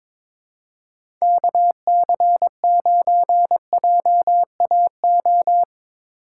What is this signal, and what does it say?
Morse code, heard as a keyed tone, about 22 wpm, 700 Hz. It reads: KC9JAO
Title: USA Call Signs - 6 Characters - 3x Character Spacing - 22wpm